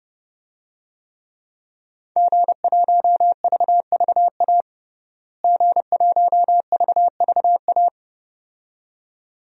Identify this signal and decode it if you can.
Morse code, heard as a keyed tone, about 32 wpm, 700 Hz. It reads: G1VVA G1VVA
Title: DXCC-100 Call Signs - 5 Characters - Repeated 2x - 30wpm